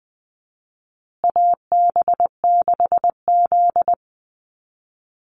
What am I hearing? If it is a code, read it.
Morse code, heard as a keyed tone, about 20 wpm, 700 Hz. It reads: AB6Z
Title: Call Signs Repeated 2x - 20wpm